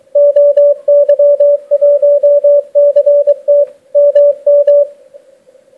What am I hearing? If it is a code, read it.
Morse code, heard as a keyed tone, about 23 wpm, 580 Hz. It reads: OY1CT MM